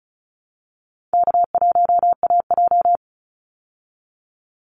Morse code, heard as a keyed tone, about 34 wpm, 700 Hz. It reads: K1AJ
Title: Call Signs Repeated 3x - 35wpm